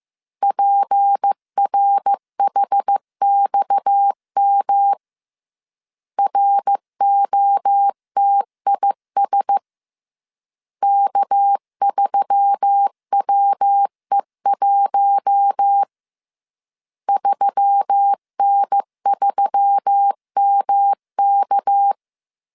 Morse code, heard as a keyed tone, about 15 wpm, 800 Hz. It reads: PRHXM ROTIS K3WE1 3N3MK
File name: CW / DET / 1 6 1500C S C 14wpm 2951